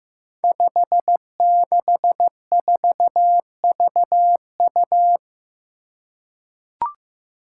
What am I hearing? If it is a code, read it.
Morse code, heard as a keyed tone, about 15 wpm, 700 Hz. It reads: 564VU
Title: Five Challenging Letter-Number Combinations 15wpm